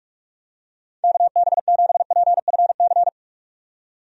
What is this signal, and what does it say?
Morse code, heard as a keyed tone, about 46 wpm, 700 Hz. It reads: KB7PFC
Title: USA Call Signs - 6 Characters - 45wpm